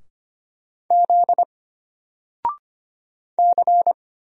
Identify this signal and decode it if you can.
Morse code, heard as a keyed tone, about 25 wpm, 700 Hz. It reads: Z C
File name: Single-Letter-Number-25wpm